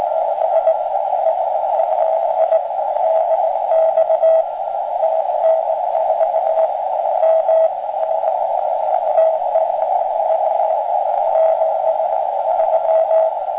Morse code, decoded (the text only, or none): none